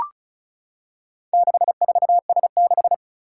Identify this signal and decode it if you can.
Morse code, heard as a keyed tone, about 36 wpm, 690 Hz. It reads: 64S6
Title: Four Challenging Letter-Number Combinations 35wpm